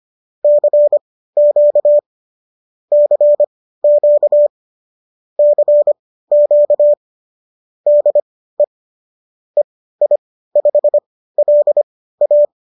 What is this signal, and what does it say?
Morse code, heard as a keyed tone, about 25 wpm, spaced as for 15 wpm, 590 Hz. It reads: CQ CQ CQ DE EI5LA